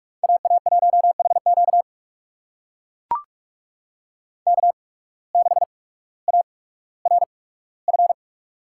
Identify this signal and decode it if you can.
Morse code, heard as a keyed tone, about 44 wpm, spaced as for 19 wpm, 700 Hz. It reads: AA1HQ K6ARF
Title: USA Call Signs - 5 Characters - 8x Character Spacing - 45wpm